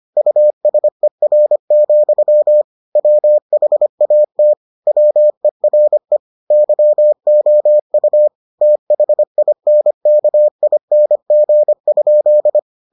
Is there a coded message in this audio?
Yes: Morse code from 0.2 to 12.6 s, heard as a keyed tone, about 25 wpm, 600 Hz. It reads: USER, WHAT WERE YOU THINKING?